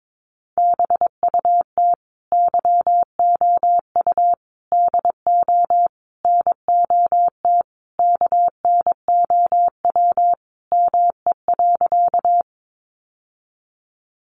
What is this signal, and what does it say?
Morse code, heard as a keyed tone, about 22 wpm, 700 Hz. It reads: BUT YOU DO NOT KNOW ME.
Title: Sentences from Top 600 Words 22wpm